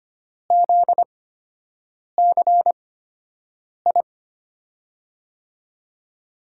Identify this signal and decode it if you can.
Morse code, heard as a keyed tone, about 25 wpm, spaced as for 7 wpm, 700 Hz. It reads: ZCI
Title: USA Call Signs - 8x Character Spacing - 25wpm